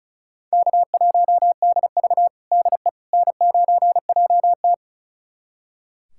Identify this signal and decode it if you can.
Morse code, heard as a keyed tone, about 34 wpm, 700 Hz. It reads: K1DV DE N9JT
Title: USA Callsign DE Callsign - 4 Characters - 35wpm